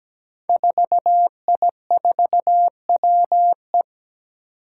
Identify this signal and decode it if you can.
Morse code, heard as a keyed tone, about 17 wpm, 700 Hz. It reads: 4I4WE
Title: Five Challenging Letter-Number Combinations 17wpm